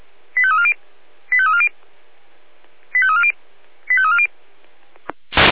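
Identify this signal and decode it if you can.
Five-tone selective call (ZVEI), heard as quick runs of five tones, tones of about 70 ms. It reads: 85340 85340 85340 85340
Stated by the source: ZVEI five-tone selective call received with an SDR.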